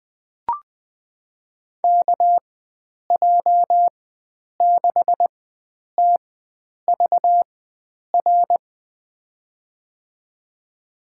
Morse code, heard as a keyed tone, about 20 wpm, spaced as for 9 wpm, 700 Hz. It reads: KJ6TVR
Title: USA Call Signs - 6 Characters - 4x Character Spacing - 20wpm